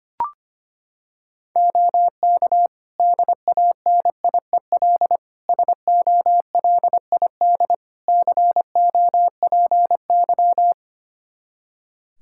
Morse code, heard as a keyed tone, about 25 wpm, 700 Hz. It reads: OK DANIEL SOLID COPY
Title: K1USN SST QSO Practice 25wpm